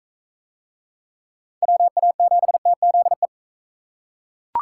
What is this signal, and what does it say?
Morse code, heard as a keyed tone, about 40 wpm, 700 Hz. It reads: WA7TZE